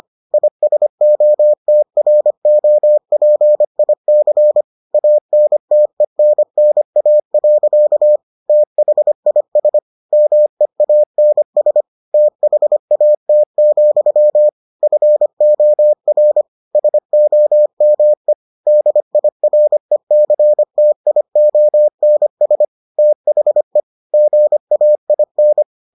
Morse code, heard as a keyed tone, about 25 wpm, 600 Hz. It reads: ISOTROPIC ANTENNA. THIS MEANS THAT, FOR SOME DIRECTIONS THE GAIN